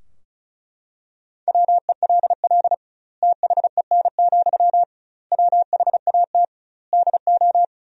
Morse code, heard as a keyed tone, about 36 wpm, 700 Hz. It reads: WELL THEN, WHAT DO
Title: 2x Word-Spacing - Sentences from Top 200 Words - 35wpm